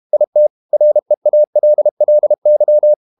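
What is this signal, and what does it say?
Morse code, heard as a keyed tone, about 32 wpm, 600 Hz. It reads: IT REALLY
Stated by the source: Morse code practice at 32 wpm from CW med Gnister, reading Ken Follett's novel Never in English.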